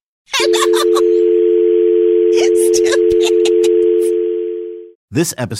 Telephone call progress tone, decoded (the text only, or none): dial tone